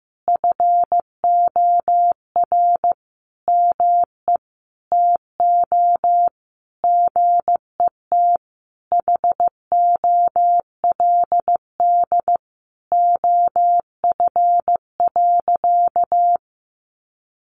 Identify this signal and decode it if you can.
Morse code, heard as a keyed tone, about 15 wpm, 700 Hz. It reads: FOR ME TO GET HOLD OF.
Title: A Sound Of Thunder 15wpm